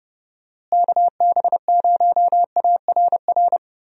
Morse code, heard as a keyed tone, about 30 wpm, 700 Hz. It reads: KB0ARR
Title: USA Call Signs - 6 Characters - Repeated 3x - 30wpm